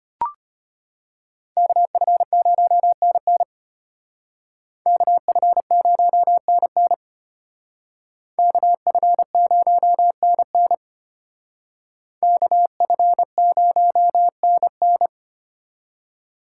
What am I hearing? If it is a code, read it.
Morse code, heard as a keyed tone, about 32 wpm, 700 Hz. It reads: KF0NN KF0NN KF0NN KF0NN